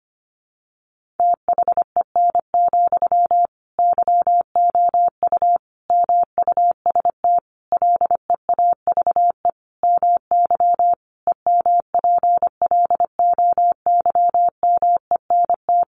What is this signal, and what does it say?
Morse code, heard as a keyed tone, about 25 wpm, 700 Hz. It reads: THEN, YOU MUST LEAVE MY EMPLOYMENT
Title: Sentences from Top 1500 Words 25wpm